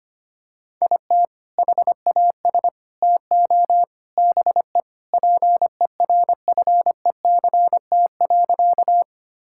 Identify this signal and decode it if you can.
Morse code, heard as a keyed tone, about 25 wpm, 700 Hz. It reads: IT HAS TO BE PERFECT.
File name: practice / 25wpm / Sentences with 4-Word Phrases - 25wpm